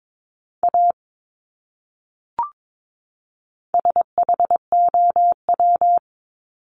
Morse code, heard as a keyed tone, about 22 wpm, 700 Hz. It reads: A SHOW
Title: Top 300 Words 22wpm